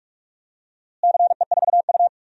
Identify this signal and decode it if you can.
Morse code, heard as a keyed tone, about 46 wpm, 700 Hz. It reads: CE4U
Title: DXCC-100 Call Signs - 4 Characters - Repeated 2x - 45wpm